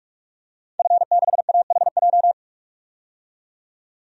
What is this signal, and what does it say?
Morse code, heard as a keyed tone, about 46 wpm, 700 Hz. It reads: F6AHJ
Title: DXCC-100 Call Signs - 5 Characters - 45wpm